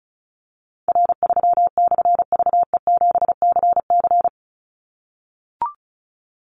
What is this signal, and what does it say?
Morse code, heard as a keyed tone, about 34 wpm, 700 Hz. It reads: R3/VE7CC
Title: Call Signs - 35wpm